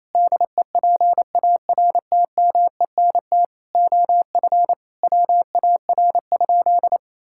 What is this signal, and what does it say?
Morse code, heard as a keyed tone, about 28 wpm, 700 Hz. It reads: DEPARTMENT OF WAR?
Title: Sentences from Top 1000 Words - Review Part 5 - 28wpm